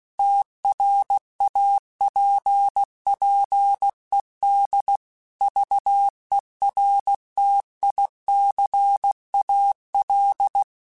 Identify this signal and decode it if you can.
Morse code, heard as a keyed tone, about 16 wpm, 800 Hz. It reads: TRAPPED VERTICAL
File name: QSOs / QSOs-mp3s / kc7haa@15